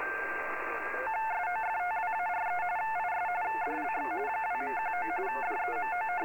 RTTY, about 45.45 baud. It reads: SOFTWARE IS DLX 2.19 DEMO